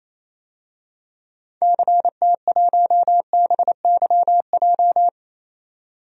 Morse code, heard as a keyed tone, about 27 wpm, 700 Hz. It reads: CT1BYJ
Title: DXCC-100 Call Signs - 6 Characters - 28wpm